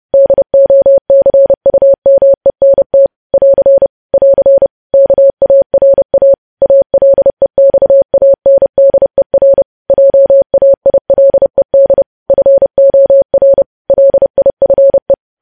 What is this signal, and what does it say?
Morse code, heard as a keyed tone, about 30 wpm, 550 Hz. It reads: DOCUMENT <AR> <AR> KARA ALEXANDER JAILED FOR LIFE
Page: news headlines in morse code at 30 wpm